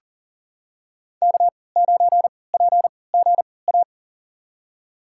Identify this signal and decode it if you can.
Morse code, heard as a keyed tone, about 40 wpm, spaced as for 23 wpm, 700 Hz. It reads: K9PGA
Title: USA Call Signs - 3x Character Spacing - 40wpm